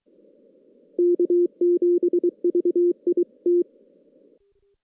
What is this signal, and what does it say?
Morse code, heard as a keyed tone, about 23 wpm, 350 Hz. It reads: K7VIT
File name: 2016 JAN CW NAQP / 2016 Jan naqp k5ot - 0-6